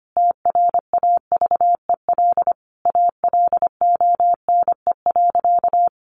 Morse code, heard as a keyed tone, about 25 wpm, 700 Hz. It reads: TRAVEL ALONE.
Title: Sentences from Top 1100 Words 25wpm